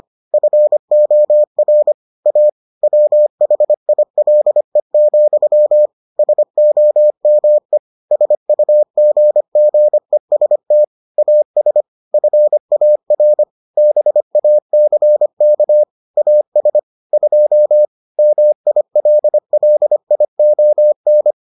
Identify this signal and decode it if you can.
Morse code, heard as a keyed tone, about 25 wpm, 600 Hz. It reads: FOR A WHILE, SOME SUGGEST AS FAR BACK AS 2 MILLION